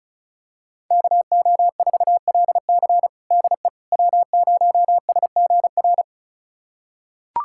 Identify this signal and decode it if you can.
Morse code, heard as a keyed tone, about 36 wpm, 700 Hz. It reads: KO4LC DE W0SGR
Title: USA Callsign DE Callsign - 5 Characters - 35wpm